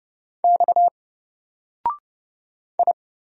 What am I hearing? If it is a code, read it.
Morse code, heard as a keyed tone, about 30 wpm, 700 Hz. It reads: X I
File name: Single-Letter-Number-30wpm